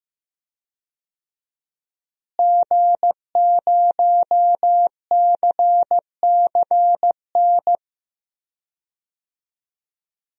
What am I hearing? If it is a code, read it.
Morse code, heard as a keyed tone, about 15 wpm, 700 Hz. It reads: G0CCN
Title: DXCC-100 Call Signs - 5 Characters - Repeated 3x - 15wpm